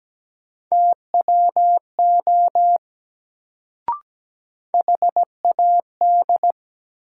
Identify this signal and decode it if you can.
Morse code, heard as a keyed tone, about 17 wpm, 700 Hz. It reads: TWO HAD
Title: Sets of 2 Words - New Word Drill - 17wpm